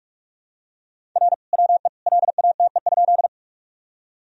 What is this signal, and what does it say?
Morse code, heard as a keyed tone, about 46 wpm, 700 Hz. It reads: R WE LATE?
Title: Sentences from Top 300 Words and Abbreviations 45wpm